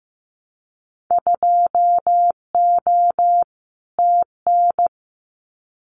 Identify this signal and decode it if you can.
Morse code, heard as a keyed tone, about 15 wpm, 700 Hz. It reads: 2O TN
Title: Winter Field Day Exchanges 15wpm